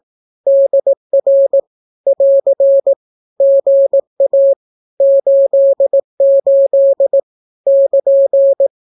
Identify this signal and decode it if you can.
Morse code, heard as a keyed tone, about 18 wpm, 550 Hz. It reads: DR <AR> GA 88 <KN>
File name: naukapodstaw / skroty / skroty